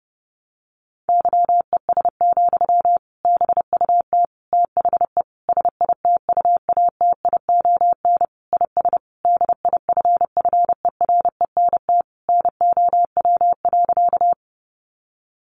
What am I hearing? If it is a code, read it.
Morse code, heard as a keyed tone, about 30 wpm, 700 Hz. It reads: YES, BUT THE SITUATION IS DIFFERENT NOW.
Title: Sentences from Top 1200 Words 30wpm